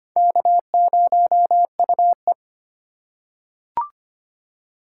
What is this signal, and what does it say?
Morse code, heard as a keyed tone, about 25 wpm, 700 Hz. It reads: K0UE
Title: USA Call Signs - 4 Characters - 4x Character Spacing - 25wpm